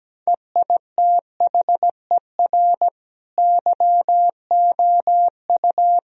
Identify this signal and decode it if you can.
Morse code, heard as a keyed tone, about 17 wpm, 700 Hz. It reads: EITHER YOU
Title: Sentences from Top 1000 Words - Review Part 4 - 17wpm